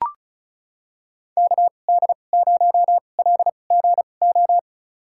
Morse code, heard as a keyed tone, about 34 wpm, spaced as for 25 wpm, 700 Hz. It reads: KD0LGO
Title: USA Call Signs - 6 Characters - 2x Character Spacing - 35wpm